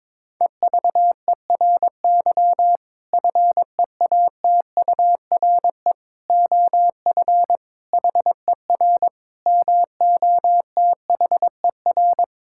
Morse code, heard as a keyed tone, about 22 wpm, 700 Hz. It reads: EVERY FEATURE OF HER MOTHER
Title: Sentences from Top 1000 Words - Review Part 5 - 22wpm